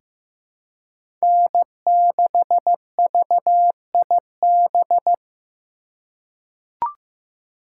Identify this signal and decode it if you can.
Morse code, heard as a keyed tone, about 15 wpm, 700 Hz. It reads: N6VIB
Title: USA Call Signs - 5 Characters - 2x Character Spacing - 15wpm